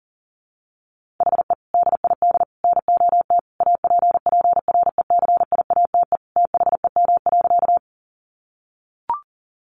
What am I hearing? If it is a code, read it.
Morse code, heard as a keyed tone, about 40 wpm, 700 Hz. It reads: HE DID NOT APPRECIATE THEM.
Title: Sentences from Top 1300 Words 40wpm